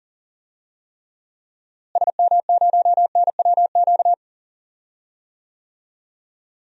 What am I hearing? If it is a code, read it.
Morse code, heard as a keyed tone, about 39 wpm, 700 Hz. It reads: SM0NWQ
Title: DXCC-100 Call Signs - 6 Characters - Repeated 3x - 40wpm